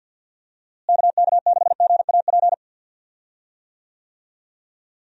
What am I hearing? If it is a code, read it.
Morse code, heard as a keyed tone, about 48 wpm, 700 Hz. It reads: KK6GAP
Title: USA Call Signs - 6 Characters - Repeated 3x - 50wpm